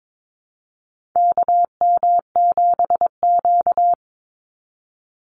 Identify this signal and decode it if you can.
Morse code, heard as a keyed tone, about 22 wpm, 700 Hz. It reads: KM7Q
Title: Call Signs - 22wpm